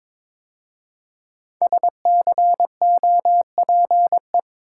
Morse code, heard as a keyed tone, about 22 wpm, 700 Hz. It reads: SCOPE
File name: 5 Letter Words - First 3 Letters Given - 22wpm